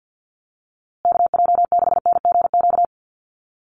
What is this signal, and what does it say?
Morse code, heard as a keyed tone, about 50 wpm, 700 Hz. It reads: KJ6NGQ
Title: USA Call Signs Repeated 2x 50wpm